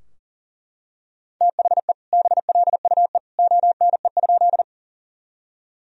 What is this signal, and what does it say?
Morse code, heard as a keyed tone, about 40 wpm, 700 Hz. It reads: THE BLUE ONE?
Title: Sentences from Top 1000 Words - Review Part 3 - 40wpm